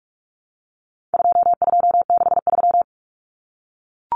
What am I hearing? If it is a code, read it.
Morse code, heard as a keyed tone, about 44 wpm, 700 Hz. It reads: 2263
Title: Common Cut Numbers - 4-Digit 45wpm